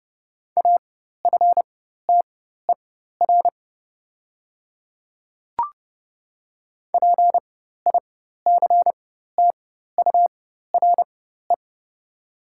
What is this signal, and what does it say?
Morse code, heard as a keyed tone, about 30 wpm, spaced as for 14 wpm, 700 Hz. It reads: AFTER PICTURE
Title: 4x Character Spacing - Top 300 Words - 30wpm